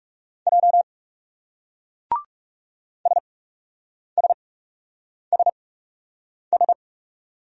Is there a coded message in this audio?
Yes: Morse code from 0.5 to 6.7 s, heard as a keyed tone, about 39 wpm, spaced as for 8 wpm, 700 Hz. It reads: J SSSS